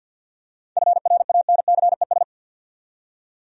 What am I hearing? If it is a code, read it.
Morse code, heard as a keyed tone, about 50 wpm, 700 Hz. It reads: FRANCES
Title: USA Common Names - 50wpm